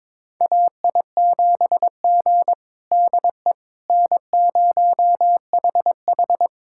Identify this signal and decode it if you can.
Morse code, heard as a keyed tone, about 22 wpm, 700 Hz. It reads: AI7G DE N0HH
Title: USA Callsign DE Callsign - 4 Characters - 22wpm